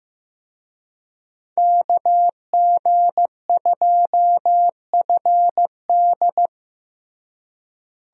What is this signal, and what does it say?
Morse code, heard as a keyed tone, about 15 wpm, 700 Hz. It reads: KG2FD
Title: USA Call Signs - 5 Characters - Repeated 2x - 15wpm